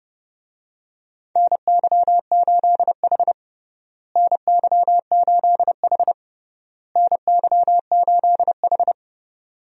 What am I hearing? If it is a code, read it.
Morse code, heard as a keyed tone, about 30 wpm, 700 Hz. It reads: NY8H NY8H NY8H
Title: USA Call Signs - 4 Characters - Repeated 3x - 30wpm